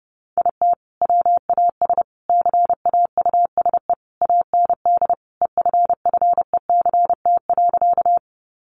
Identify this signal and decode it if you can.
Morse code, heard as a keyed tone, about 30 wpm, 700 Hz. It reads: IT WAS CAUSE AND EFFECT.
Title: Sentences from Top 1000 Words 30wpm